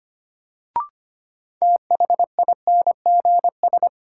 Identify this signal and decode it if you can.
Morse code, heard as a keyed tone, about 25 wpm, 700 Hz. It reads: THINGS
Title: Sentences from Top 1400 Words - New Word Drill Alt - 25wpm